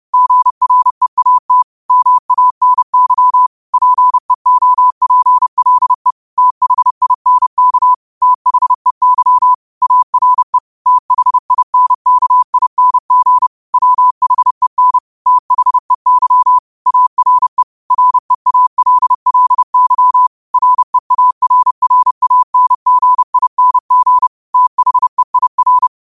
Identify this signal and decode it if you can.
Morse code, heard as a keyed tone, about 30 wpm, 1000 Hz. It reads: GREAT MANY PEOPLE THINK THEY ARE THINKING WHEN THEY ARE REALLY REARRANGING THEIR